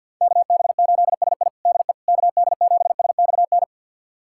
Morse code, heard as a keyed tone, about 51 wpm, 700 Hz. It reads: KB8SI DE KD7SXN
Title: USA Callsign DE Callsign 50wpm